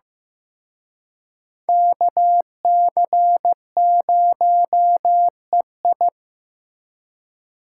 Morse code, heard as a keyed tone, about 18 wpm, 700 Hz. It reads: KC0EI
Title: Speed Racing - USA 2x2 Call Signs 15wpm